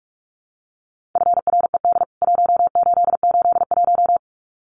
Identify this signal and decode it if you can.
Morse code, heard as a keyed tone, about 44 wpm, 700 Hz. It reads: FRED 1881
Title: CWT Exchange 45wpm